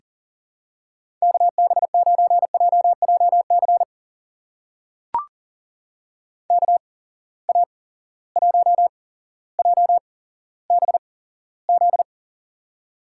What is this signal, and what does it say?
Morse code, heard as a keyed tone, about 40 wpm, spaced as for 17 wpm, 700 Hz. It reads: KB9JJC KA1JBZ
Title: USA Call Signs - 6 Characters - 8x Character Spacing - 40wpm